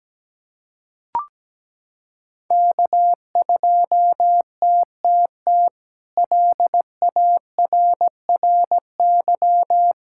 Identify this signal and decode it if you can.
Morse code, heard as a keyed tone, about 17 wpm, 700 Hz. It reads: K2TTT LARRY
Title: K1USN SST QSO Practice - 17wpm